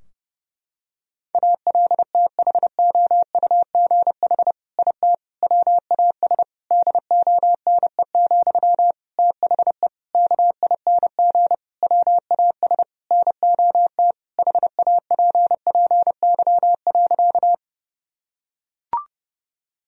Morse code, heard as a keyed tone, about 32 wpm, 700 Hz. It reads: ALTHOUGH IT WAS DONE, THE KING WAS NOT HAPPY.
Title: Sentences from Top 1000 Words - Review Part 3 - 30wpm